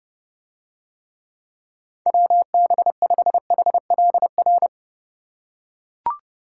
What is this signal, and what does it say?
Morse code, heard as a keyed tone, about 31 wpm, 700 Hz. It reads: WB5HLR